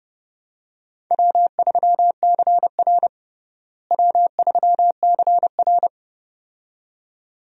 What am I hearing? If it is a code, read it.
Morse code, heard as a keyed tone, about 32 wpm, 700 Hz. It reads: W3CR W3CR
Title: USA Call Signs - 4 Characters - Repeated 2x - 30wpm